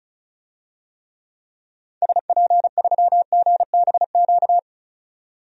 Morse code, heard as a keyed tone, about 36 wpm, 700 Hz. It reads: SP3GBQ
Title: DXCC-100 Call Signs - 6 Characters - 35wpm